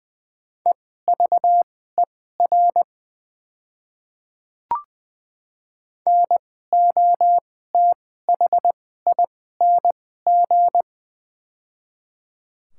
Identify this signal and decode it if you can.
Morse code, heard as a keyed tone, about 21 wpm, spaced as for 15 wpm, 690 Hz. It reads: EVER NOTHING
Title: 2x Character Spacing - Top 500 Words - 20wpm